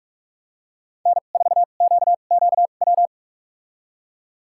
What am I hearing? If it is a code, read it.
Morse code, heard as a keyed tone, about 46 wpm, spaced as for 32 wpm, 700 Hz. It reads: N4QQW
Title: USA Call Signs - 5 Characters - 2x Character Spacing - 45wpm